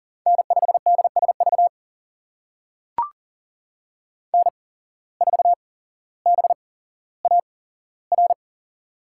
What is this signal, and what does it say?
Morse code, heard as a keyed tone, about 40 wpm, spaced as for 17 wpm, 700 Hz. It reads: N5DSV N4BAR
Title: USA Call Signs - 5 Characters - 8x Character Spacing - 40wpm